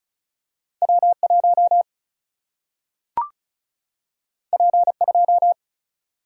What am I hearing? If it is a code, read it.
Morse code, heard as a keyed tone, about 36 wpm, 700 Hz. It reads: W 1 P 2
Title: Single-Letter Single-Number 35wpm